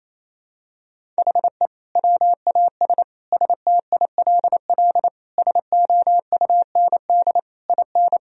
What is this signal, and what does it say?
Morse code, heard as a keyed tone, about 28 wpm, 700 Hz. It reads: HE WAS STILL SOUND IN
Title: Sentences from Top 1000 Words - Review Part 2 - 28wpm